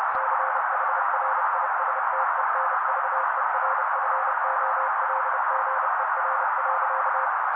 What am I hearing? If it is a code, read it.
Morse code, heard as a keyed tone, about 30 wpm, 530 Hz. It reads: CHRISTENVERFOLGER.
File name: daily / 29wpm noise